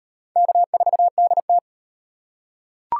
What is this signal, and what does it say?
Morse code, heard as a keyed tone, about 35 wpm, 700 Hz. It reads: K4DT